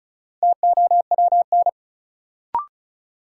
Morse code, heard as a keyed tone, about 36 wpm, 700 Hz. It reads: TOWN
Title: Sentences from Top 500 Words - New Word Drill - 35wpm